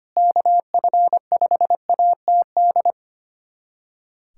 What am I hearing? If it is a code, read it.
Morse code, heard as a keyed tone, about 25 wpm, 700 Hz. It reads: KF5ATD
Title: USA Callsign DE Callsign - 6 Characters - 25wpm